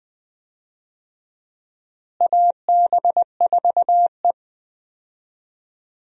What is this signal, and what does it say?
Morse code, heard as a keyed tone, about 20 wpm, 700 Hz. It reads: AB4E
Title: USA Call Signs - 4 Characters - Repeated 3x - 20wpm